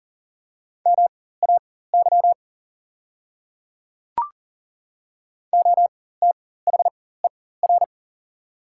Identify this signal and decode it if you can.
Morse code, heard as a keyed tone, about 41 wpm, spaced as for 19 wpm, 700 Hz. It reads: MAY OTHER
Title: 4x Character Spacing - Top 200 Words - 40wpm